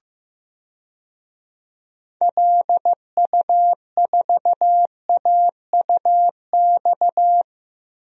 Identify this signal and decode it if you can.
Morse code, heard as a keyed tone, about 15 wpm, 700 Hz. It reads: LU4AUX
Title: DXCC-100 Call Signs - 6 Characters - Repeated 2x - 15wpm